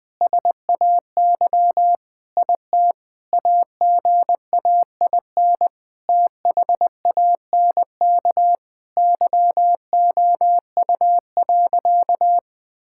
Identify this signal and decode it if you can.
Morse code, heard as a keyed tone, about 20 wpm, 700 Hz. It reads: SAY IT AGAIN THANK YOU.
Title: Sentences with 8-Word Phrases - 20wpm